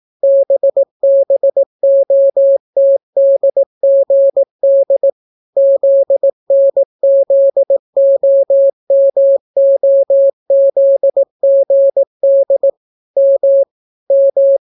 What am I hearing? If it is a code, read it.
Morse code, heard as a keyed tone, about 18 wpm, 550 Hz. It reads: BBOTDGD ZNZOMOZGD M M